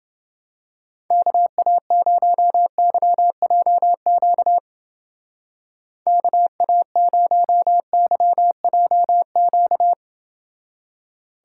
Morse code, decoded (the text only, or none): KA0YJQ KA0YJQ